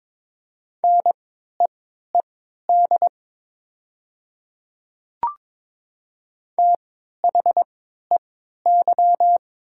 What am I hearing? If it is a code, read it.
Morse code, heard as a keyed tone, about 22 wpm, spaced as for 12 wpm, 700 Hz. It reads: NEED THEY
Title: 3x Character Spacing - Top 300 Words - 22wpm